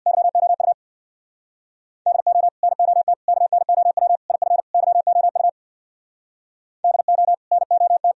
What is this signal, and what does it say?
Morse code, heard as a keyed tone, about 62 wpm, 700 Hz. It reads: YOU DO NOT KNOW IF YOU DO NOT
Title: Speed-Racing - Sets of 5 Words 50wpm